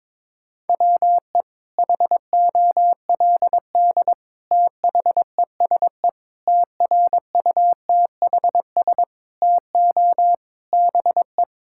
Morse code, heard as a keyed tone, about 22 wpm, 700 Hz. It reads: WE HOLD THESE TRUTHS TO BE
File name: Sets of 9 Words - 22wpm